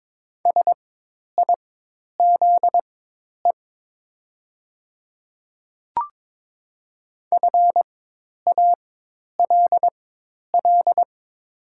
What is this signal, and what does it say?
Morse code, heard as a keyed tone, about 22 wpm, spaced as for 10 wpm, 700 Hz. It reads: SIZE FALL
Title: 4x Character Spacing - Top 500 Words - 22wpm